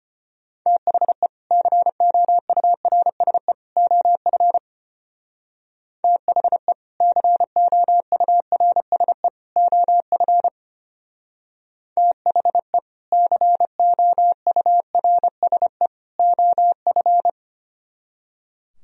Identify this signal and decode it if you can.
Morse code, heard as a keyed tone, about 29 wpm, 700 Hz. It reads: THE COURSE OF THE COURSE OF THE COURSE OF
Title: Speed-Racing - Sets of 3 Words 25wpm